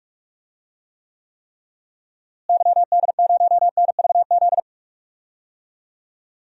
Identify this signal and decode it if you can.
Morse code, heard as a keyed tone, about 46 wpm, 700 Hz. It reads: YD0NVZ
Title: DXCC-100 Call Signs - 6 Characters - Repeated 3x - 45wpm